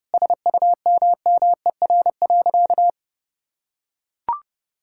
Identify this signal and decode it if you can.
Morse code, heard as a keyed tone, about 32 wpm, 700 Hz. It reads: SUMMER.
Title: Sentences from Top 1000 Words - Review Part 6 - 30wpm